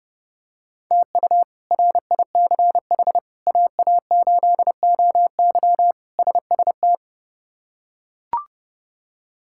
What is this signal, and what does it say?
Morse code, heard as a keyed tone, about 32 wpm, 700 Hz. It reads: TU RICH AA8OY SST
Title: K1USN SST QSO Practice - 30wpm